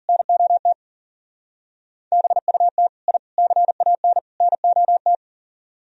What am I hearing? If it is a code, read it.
Morse code, heard as a keyed tone, about 49 wpm, 700 Hz. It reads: NOT BUT I CAN NOT
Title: Speed-Racing - Sets of 4 Words 40wpm